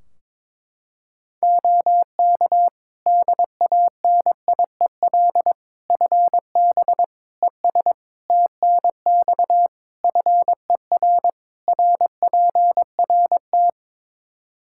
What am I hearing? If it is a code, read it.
Morse code, heard as a keyed tone, about 22 wpm, 700 Hz. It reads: OK DANIEL FB ES TNX FER RPRT
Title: K1USN SST QSO Practice 22wpm